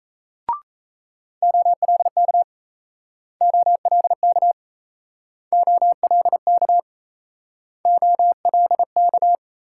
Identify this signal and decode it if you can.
Morse code, heard as a keyed tone, about 37 wpm, 700 Hz. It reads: OLK OLK OLK OLK